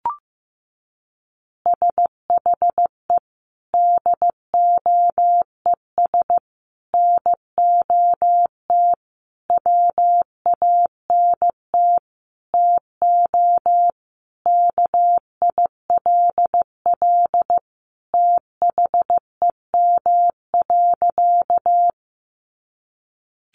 Morse code, heard as a keyed tone, about 15 wpm, 700 Hz. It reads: SHE DOES NOT WANT TO KILL THEM.
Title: Additional Easy Phrases 15wpm